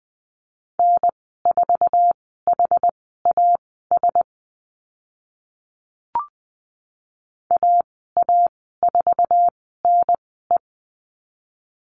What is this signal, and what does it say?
Morse code, heard as a keyed tone, about 20 wpm, spaced as for 15 wpm, 700 Hz. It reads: N4HAS AA4NE
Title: 062 Lesson 11g - Letter h - Call Signs 20wpm